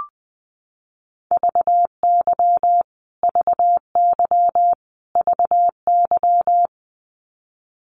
Morse code, heard as a keyed tone, about 20 wpm, 700 Hz. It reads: VY VY VY
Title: Abbreviations Repeated 3x - 20wpm